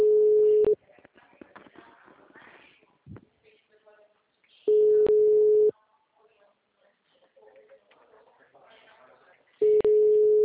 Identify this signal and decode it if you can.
Ringback tone (Europe, 425 Hz), with speech.
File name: out- -2002- - - .59967